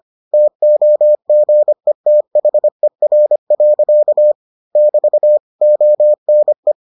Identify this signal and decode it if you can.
Morse code, heard as a keyed tone, about 25 wpm, 600 Hz. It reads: TOGETHER. <BT> ONE